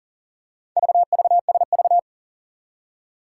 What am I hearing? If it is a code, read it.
Morse code, heard as a keyed tone, about 44 wpm, 700 Hz. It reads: VVSV
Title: Four Challenging Letter-Number Combinations 40wpm